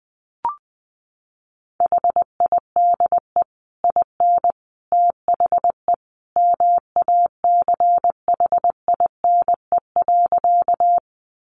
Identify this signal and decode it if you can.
Morse code, heard as a keyed tone, about 20 wpm, 700 Hz. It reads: HIDE IN THE MACHINE.
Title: A Sound Of Thunder 20wpm